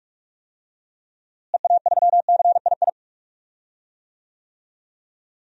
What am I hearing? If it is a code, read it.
Morse code, heard as a keyed tone, about 46 wpm, 700 Hz. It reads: EA3CII
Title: DXCC-100 Call Signs - 6 Characters - Repeated 2x - 45wpm